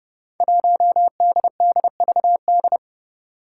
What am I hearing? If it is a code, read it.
Morse code, heard as a keyed tone, about 30 wpm, 700 Hz. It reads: 1DDVD
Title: Five Challenging Letter-Number Combinations 30wpm